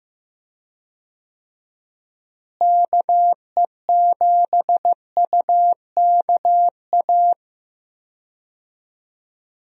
Morse code, heard as a keyed tone, about 15 wpm, 700 Hz. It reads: KE7UKA
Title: USA Call Signs - 6 Characters - Repeated 2x - 15wpm